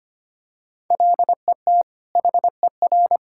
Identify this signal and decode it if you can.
Morse code, heard as a keyed tone, about 25 wpm, 700 Hz. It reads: LET HER
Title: Sentences from Top 1000 Words - Review Part 6 - 25wpm